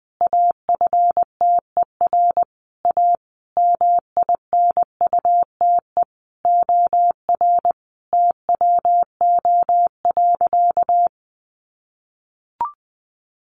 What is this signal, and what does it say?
Morse code, heard as a keyed tone, about 20 wpm, 700 Hz. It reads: AFTER A MINUTE OR TWO.
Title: Sentences from Top 1400 Words 20wpm